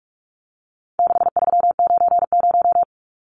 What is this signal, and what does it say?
Morse code, heard as a keyed tone, about 44 wpm, 700 Hz. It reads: 6390
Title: Numbers - 4-Digit 45wpm